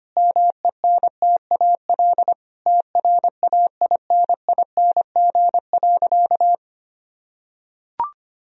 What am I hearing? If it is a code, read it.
Morse code, heard as a keyed tone, about 25 wpm, 700 Hz. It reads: MENTAL TRAINING.
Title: Sentences from Top 1000 Words - Review Part 4 - 25wpm